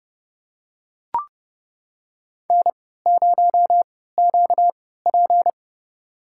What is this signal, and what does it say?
Morse code, heard as a keyed tone, about 30 wpm, spaced as for 17 wpm, 700 Hz. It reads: N0QP
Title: USA Call Signs - 4 Characters - 3x Character Spacing - 30wpm